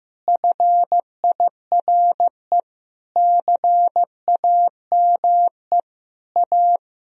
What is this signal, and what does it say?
Morse code, heard as a keyed tone, about 15 wpm, 700 Hz. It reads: FIRE CAME A
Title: Sentences from Top 1000 Words - Review Part 3 - 15wpm